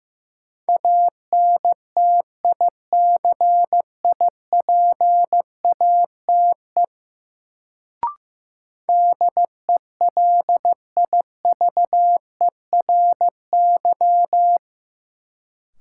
Morse code, heard as a keyed tone, about 16 wpm, 700 Hz. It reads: ANTICIPATE DELIVERY
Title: Sentences from Top 1500 Words - New Word Drill - 15wpm